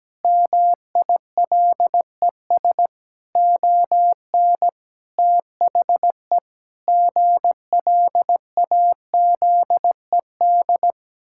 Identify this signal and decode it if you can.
Morse code, heard as a keyed tone, about 17 wpm, 700 Hz. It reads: MILES ON THE GLAZED
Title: Pangram Challenge v2 - 17wpm